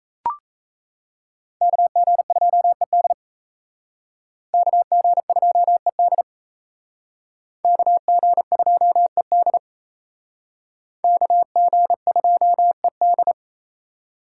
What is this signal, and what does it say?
Morse code, heard as a keyed tone, about 37 wpm, 700 Hz. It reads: KG2ED KG2ED KG2ED KG2ED